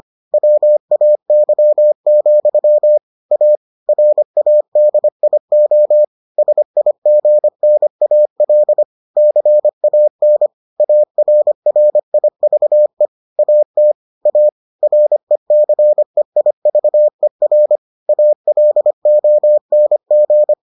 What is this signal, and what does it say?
Morse code, heard as a keyed tone, about 25 wpm, 600 Hz. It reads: WAY, A RADIO SIGNAL CAN ARRIVE AT A RECEIVER ALONG